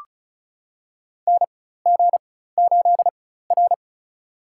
Morse code, heard as a keyed tone, about 36 wpm, spaced as for 16 wpm, 700 Hz. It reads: NG8R
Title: USA Call Signs - 4 Characters - 4x Character Spacing - 35wpm